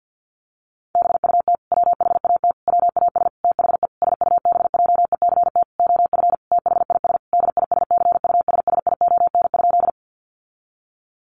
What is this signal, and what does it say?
Morse code, heard as a keyed tone, about 50 wpm, 700 Hz. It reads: BUT WHAT WAS THE SUBJECT OF THIS DISCUSSION?
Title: Sentences from Top 1100 Words 50wpm